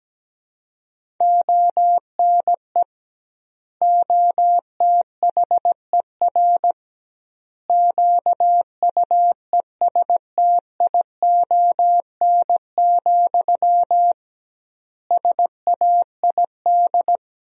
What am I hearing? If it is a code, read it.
Morse code, heard as a keyed tone, about 17 wpm, 700 Hz. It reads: ONE OTHER QUESTION, SAID
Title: 2x Word-Spacing - Sentences from Top 400 Words - 17wpm